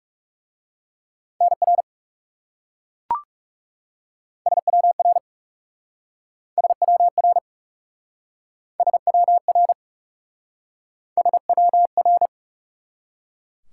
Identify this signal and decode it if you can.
Morse code, heard as a keyed tone, about 40 wpm, 700 Hz. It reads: NR SWR SWR SWR SWR